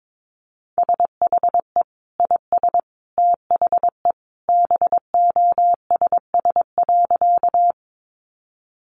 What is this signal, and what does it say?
Morse code, heard as a keyed tone, about 22 wpm, 700 Hz. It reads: SHE IS THE BOSS.
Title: Sentences from Top 700 Words 22wpm